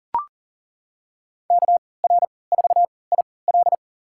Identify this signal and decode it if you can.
Morse code, heard as a keyed tone, about 40 wpm, spaced as for 23 wpm, 700 Hz. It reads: KR4IL
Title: USA Call Signs - 5 Characters - 3x Character Spacing - 40wpm